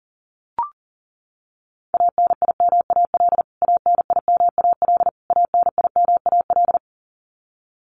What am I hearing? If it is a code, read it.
Morse code, heard as a keyed tone, about 40 wpm, 700 Hz. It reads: ANIMAL ANIMAL ANIMAL
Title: Top 200 Words Repeated 40wpm